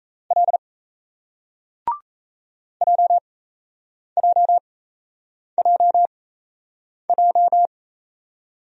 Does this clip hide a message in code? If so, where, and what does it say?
Yes: Morse code from 0.3 to 7.7 s, heard as a keyed tone, about 38 wpm, spaced as for 8 wpm, 700 Hz. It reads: L JJJJ